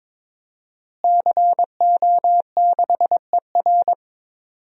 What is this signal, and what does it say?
Morse code, heard as a keyed tone, about 22 wpm, 700 Hz. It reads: CO6ER
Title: DXCC-100 Call Signs - 5 Characters - Repeated 3x - 22wpm